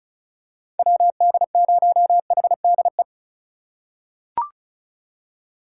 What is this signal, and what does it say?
Morse code, heard as a keyed tone, about 36 wpm, 690 Hz. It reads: WD0HDE
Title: USA Call Signs - 6 Characters - 6x Character Spacing - 35wpm